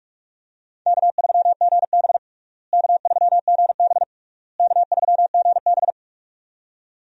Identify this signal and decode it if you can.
Morse code, heard as a keyed tone, about 46 wpm, 700 Hz. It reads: K3GB K3GB K3GB
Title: USA Call Signs - 4 Characters - Repeated 3x - 45wpm